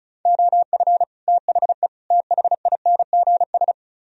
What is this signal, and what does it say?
Morse code, heard as a keyed tone, about 46 wpm, 700 Hz. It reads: OF THE THINGS